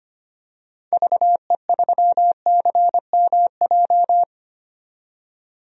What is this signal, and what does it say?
Morse code, heard as a keyed tone, about 25 wpm, 700 Hz. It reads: VE3CMJ
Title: DXCC-100 Call Signs - 6 Characters - 25wpm